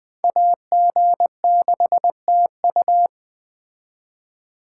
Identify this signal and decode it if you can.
Morse code, heard as a keyed tone, about 20 wpm, 700 Hz. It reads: AG6TU
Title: Speed Racing - USA 2x2 Call Signs 17wpm